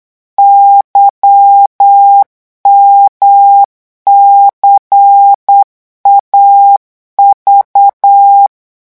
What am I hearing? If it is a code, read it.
Morse code, heard as a keyed tone, about 8 wpm, 800 Hz. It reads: YMCAV
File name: morse mp3 / revision3-vitesse 8 mots